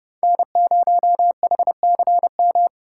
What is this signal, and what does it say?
Morse code, heard as a keyed tone, about 30 wpm, 700 Hz. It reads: N0HCM
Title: USA Call Signs - 5 Characters - 3x Character Spacing - 30wpm